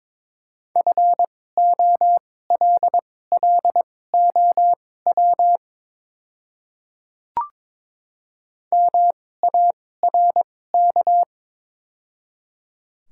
Morse code, heard as a keyed tone, about 22 wpm, spaced as for 16 wpm, 700 Hz. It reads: FOLLOW MARK
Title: 2x Character Spacing - Top 300 Words - 22wpm